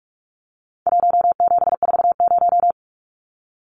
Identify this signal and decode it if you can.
Morse code, heard as a keyed tone, about 44 wpm, 700 Hz. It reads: 1740
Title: Numbers - 4-Digit 45wpm